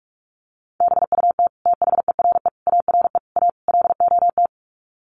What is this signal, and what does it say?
Morse code, heard as a keyed tone, about 44 wpm, 700 Hz. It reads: BUT THERE ARE A LOT
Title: Sets of 5 Words Encore2 45wpm